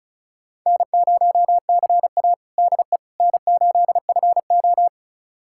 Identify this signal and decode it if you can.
Morse code, heard as a keyed tone, about 36 wpm, 700 Hz. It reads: N0CA DE N8FO
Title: USA Callsign DE Callsign - 4 Characters - 35wpm